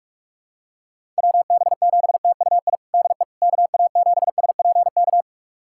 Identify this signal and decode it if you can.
Morse code, heard as a keyed tone, about 44 wpm, 700 Hz. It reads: WB7TUI DE KA7SPK
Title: USA Callsign DE Callsign 45wpm